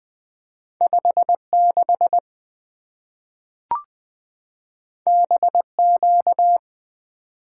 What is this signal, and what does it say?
Morse code, heard as a keyed tone, about 20 wpm, 700 Hz. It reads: 5 6 B Q
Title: Two Letter-Number Combinations - 20wpm